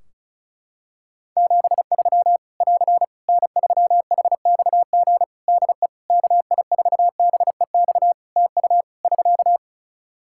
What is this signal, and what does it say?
Morse code, heard as a keyed tone, about 36 wpm, 700 Hz. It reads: 73 <AR> N3HXG DE KI4BEX TU <SK>
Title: K1USN SST QSO Practice 35wpm